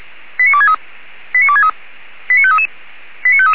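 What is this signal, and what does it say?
Five-tone selective call (ZVEI), heard as quick runs of five tones, tones of about 70 ms. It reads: 89263 89263 89533 89533
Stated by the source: Five-tone ZVEI selective call received by SDR, from the audio server of the Feuerwehr Orlamünde fire brigade.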